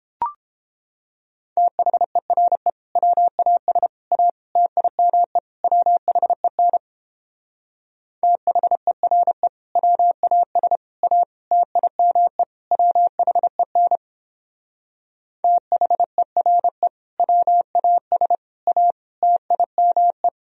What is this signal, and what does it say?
Morse code, heard as a keyed tone, about 29 wpm, 700 Hz. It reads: THERE WAS A TIME WHEN THERE WAS A TIME WHEN THERE WAS A TIME